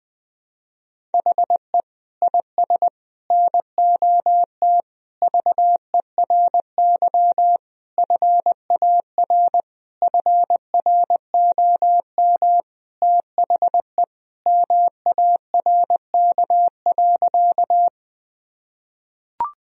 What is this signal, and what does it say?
Morse code, heard as a keyed tone, about 21 wpm, 700 Hz. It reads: HE IS NOT VERY FAR FROM THE MARK.
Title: Sentences from Top 1000 Words - Review Part 1 - 20wpm